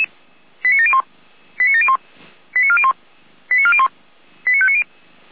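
Five-tone selective call (ZVEI), heard as quick runs of five tones, tones of about 70 ms. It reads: 89701 89701 89401 89401 89590
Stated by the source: ZVEI five-tone selective call 89356 received by SDR.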